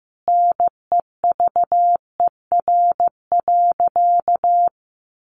Morse code, heard as a keyed tone, about 15 wpm, 700 Hz. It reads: NEVER.
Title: Sentences from Top 300 Words 15wpm